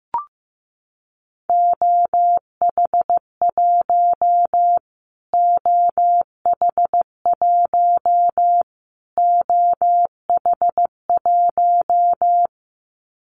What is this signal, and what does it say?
Morse code, heard as a keyed tone, about 15 wpm, 700 Hz. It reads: OH1 OH1 OH1
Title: Callsign Prefixes for Top 25 DXCC - Repeated 3x - 15wpm